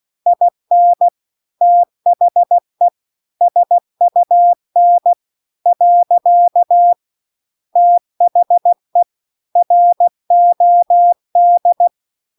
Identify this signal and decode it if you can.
Morse code, heard as a keyed tone, about 16 wpm, 700 Hz. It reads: IN THE SUN . THE ROD
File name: CW-The-Big-Twoheaded-River-2-16WPM